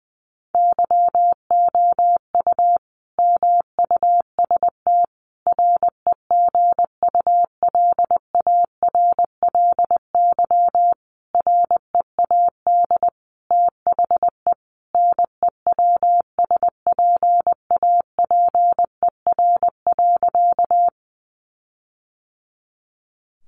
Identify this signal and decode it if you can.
Morse code, heard as a keyed tone, about 20 wpm, 700 Hz. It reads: YOU MUST REGULARLY READ THE NEWSPAPER.
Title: Sentences from Top 1400 Words 20wpm